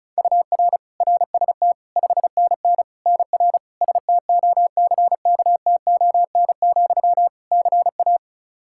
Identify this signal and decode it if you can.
Morse code, heard as a keyed tone, about 36 wpm, 700 Hz. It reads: UR RST 5NN NR STOCKTON, CA
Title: Practice QSO - Exchange 1 - Beginning - 35wpm